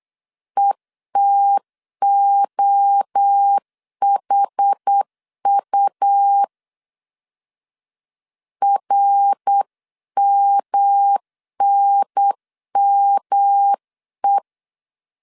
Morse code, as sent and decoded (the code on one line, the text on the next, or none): . - --- .... ..- / .-. -- -. -- .
ETOHU RMNME